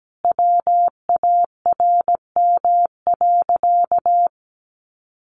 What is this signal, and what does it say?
Morse code, heard as a keyed tone, about 17 wpm, 700 Hz. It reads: WARM.
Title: Sentences from Top 1500 Words 17wpm